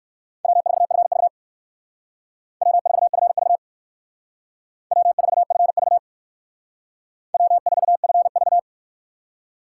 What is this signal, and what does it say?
Morse code, heard as a keyed tone, about 56 wpm, 700 Hz. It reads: W4FV W4FV W4FV W4FV